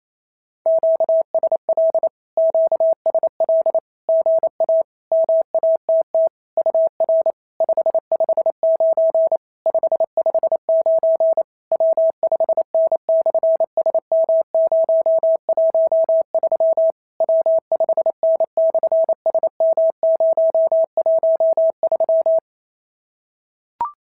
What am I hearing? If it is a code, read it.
Morse code, heard as a keyed tone, about 28 wpm, 650 Hz. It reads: QSL QSL GA MATT UR 559 559 W5N/SM013 W5N/SM013
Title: SOTA QSO Practice 28wpm